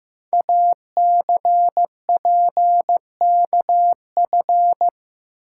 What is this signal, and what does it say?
Morse code, heard as a keyed tone, about 15 wpm, 700 Hz. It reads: ACPKF
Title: Code Groups - 5 Letter:Numbers - 15wpm